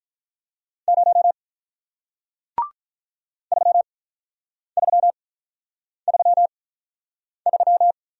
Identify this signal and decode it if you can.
Morse code, heard as a keyed tone, about 48 wpm, spaced as for 9 wpm, 700 Hz. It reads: 0 3333